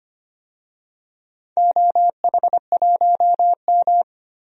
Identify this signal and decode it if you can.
Morse code, heard as a keyed tone, about 25 wpm, 700 Hz. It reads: OH1M
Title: DXCC-100 Call Signs - 4 Characters - Repeated 3x - 25wpm